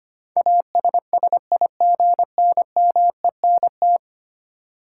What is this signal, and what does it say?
Morse code, heard as a keyed tone, about 25 wpm, 690 Hz. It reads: ASSIGNMENT
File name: 10 Letter Words - First 3 Letters Given - 25wpm